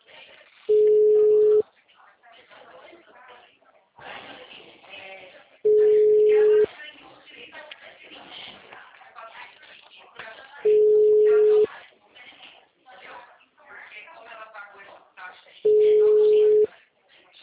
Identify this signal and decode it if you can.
Ringback tone (Europe, 425 Hz), with speech.